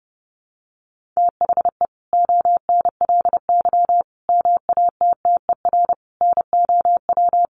Morse code, heard as a keyed tone, about 30 wpm, 700 Hz. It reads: THE ONLY MATTER NOW
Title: Sentences from Top 1400 Words 30wpm